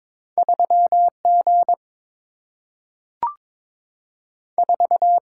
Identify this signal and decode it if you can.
Morse code, heard as a keyed tone, about 22 wpm, 690 Hz. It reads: 3G 4
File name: practice / 22wpm / Two Letter-Number Combinations - 22wpm